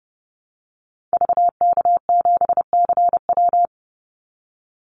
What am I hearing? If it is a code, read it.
Morse code, heard as a keyed tone, about 30 wpm, 700 Hz. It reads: VK7CW
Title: Call Signs Repeated 2x - 30wpm